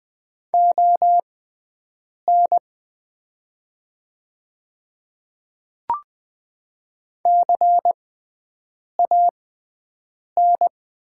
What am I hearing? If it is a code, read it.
Morse code, heard as a keyed tone, about 20 wpm, spaced as for 7 wpm, 700 Hz. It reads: ON CAN
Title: 6x Character Spacing - Top 100 Words - 20wpm